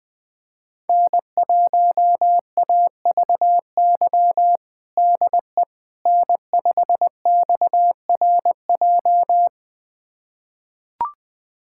Morse code, heard as a keyed tone, about 20 wpm, 700 Hz. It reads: N1AVY DE N5XRJ
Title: USA Callsign DE Callsign - 5 Characters - 20wpm